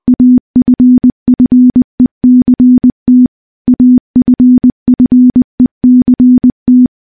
Morse code, heard as a keyed tone, about 20 wpm, 250 Hz. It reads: AFFECT AFFECT
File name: morse-training-20WPM